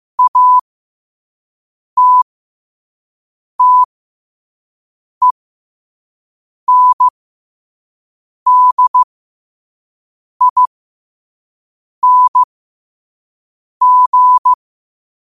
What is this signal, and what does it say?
Morse code, heard as a keyed tone, about 15 wpm, spaced as for 5 wpm, 1000 Hz. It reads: ATTENDING
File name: qotdfiles / 5wpm